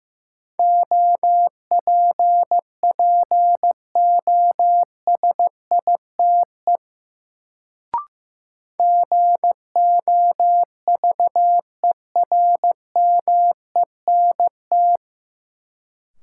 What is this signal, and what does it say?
Morse code, heard as a keyed tone, about 15 wpm, 700 Hz. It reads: OPPOSITE GOVERMENT
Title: Sentences from Top 1300 Words - New Word Drill - 15wpm